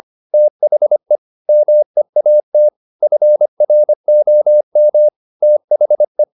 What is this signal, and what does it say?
Morse code, heard as a keyed tone, about 25 wpm, 600 Hz. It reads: THE MEAT FROM THE